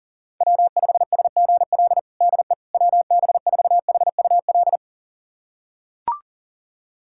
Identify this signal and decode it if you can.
Morse code, heard as a keyed tone, about 40 wpm, 700 Hz. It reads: W5SGL DE WB4HUL
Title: USA Callsign DE Callsign 40wpm